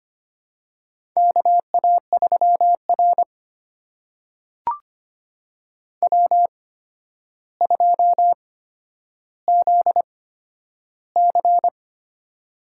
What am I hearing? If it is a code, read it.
Morse code, heard as a keyed tone, about 25 wpm, spaced as for 11 wpm, 700 Hz. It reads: KA3R W2ZC
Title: USA Call Signs - 4 Characters - 8x Character Spacing - 25wpm